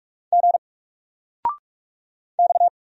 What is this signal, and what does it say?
Morse code, heard as a keyed tone, about 44 wpm, 700 Hz. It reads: G X
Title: Single Letter-Number Alt 45wpm